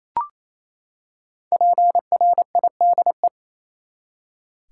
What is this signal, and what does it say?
Morse code, heard as a keyed tone, about 28 wpm, 700 Hz. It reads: PRIDE
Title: Five Letter Words - 28wpm